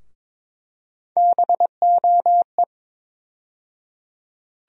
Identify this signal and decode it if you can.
Morse code, heard as a keyed tone, about 22 wpm, 700 Hz. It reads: BOE
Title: USA Call Sign Postfixes - 3 Letters 22wpm